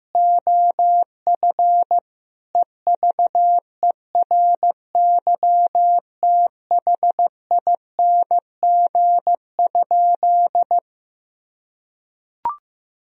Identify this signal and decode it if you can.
Morse code, heard as a keyed tone, about 15 wpm, 700 Hz. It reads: OF EVERYTHING?
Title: Sentences from Top 1000 Words - Review Part 2 - 15wpm